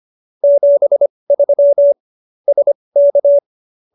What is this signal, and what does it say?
Morse code, heard as a keyed tone, about 25 wpm, spaced as for 20 wpm, 570 Hz. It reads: 73 SK